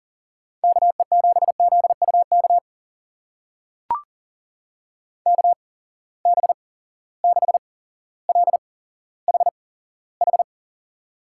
Morse code, heard as a keyed tone, about 40 wpm, spaced as for 17 wpm, 700 Hz. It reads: KE7ZUK KB6LHH
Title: USA Call Signs - 6 Characters - 8x Character Spacing - 40wpm